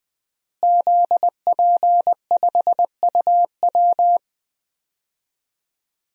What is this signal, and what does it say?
Morse code, heard as a keyed tone, about 20 wpm, 700 Hz. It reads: ZP5UW
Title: DXCC-100 Call Signs - 5 Characters - Repeated 2x - 20wpm